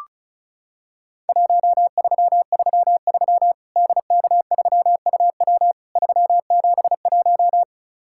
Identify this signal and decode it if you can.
Morse code, heard as a keyed tone, about 36 wpm, 700 Hz. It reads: 1333 DK3UW 371
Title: Worked All Europe DX Contest - QTC Exchange - 35wpm